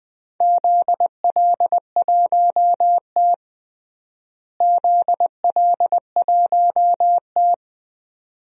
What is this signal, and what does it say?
Morse code, heard as a keyed tone, about 21 wpm, 700 Hz. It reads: ZL1T ZL1T
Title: DXCC-100 Call Signs - 4 Characters - Repeated 2x - 20wpm